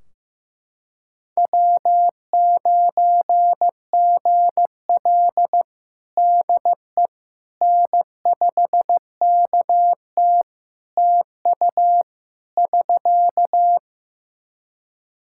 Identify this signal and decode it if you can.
Morse code, heard as a keyed tone, about 15 wpm, 700 Hz. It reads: W9GL DE N5KT TU <SK>
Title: Practice QSO - Exchange 4 - Ending - 15wpm